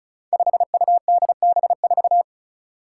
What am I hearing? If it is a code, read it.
Morse code, heard as a keyed tone, about 36 wpm, 700 Hz. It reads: 5UDB4
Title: Five Challenging Letter-Number Combinations 35wpm